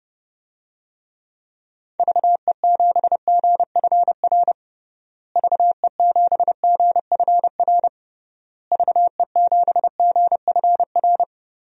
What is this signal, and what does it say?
Morse code, heard as a keyed tone, about 32 wpm, 700 Hz. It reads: VE7GFR VE7GFR VE7GFR
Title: DXCC-100 Call Signs - 6 Characters - Repeated 3x - 30wpm